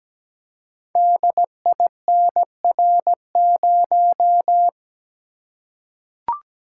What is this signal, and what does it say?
Morse code, heard as a keyed tone, about 17 wpm, 700 Hz. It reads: DINR0
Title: Code Groups - 5 Letter:Numbers - 17wpm